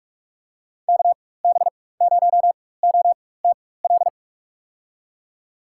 Morse code, heard as a keyed tone, about 46 wpm, spaced as for 21 wpm, 700 Hz. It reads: KB0OTL
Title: USA Call Signs - 4x Character Spacing - 45wpm